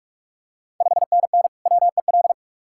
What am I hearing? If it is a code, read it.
Morse code, heard as a keyed tone, about 45 wpm, 690 Hz. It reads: 5NN WEL
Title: Ontario QSO Party Exchange 45wpm